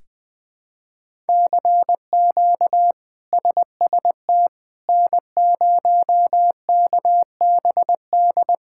Morse code, heard as a keyed tone, about 20 wpm, 700 Hz. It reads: CQ SST N0KBD
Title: K1USN SST QSO Practice - 20wpm